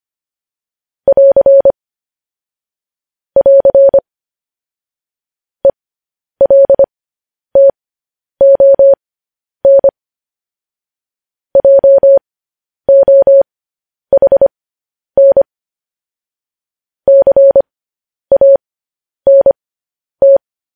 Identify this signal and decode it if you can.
Morse code, heard as a keyed tone, about 25 wpm, spaced as for 10 wpm, 550 Hz. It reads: <AR> <AR> ELTON JOHN CANT
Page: news headlines in morse code at 10 wpm